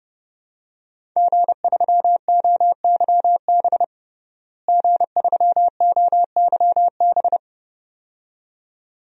Morse code, heard as a keyed tone, about 32 wpm, 700 Hz. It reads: G3OYB G3OYB
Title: DXCC-100 Call Signs - 5 Characters - Repeated 2x - 30wpm